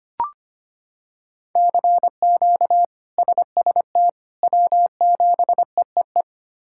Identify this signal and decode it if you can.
Morse code, heard as a keyed tone, about 25 wpm, 700 Hz. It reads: CQ SST W7EEE
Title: K1USN SST QSO Practice - 25wpm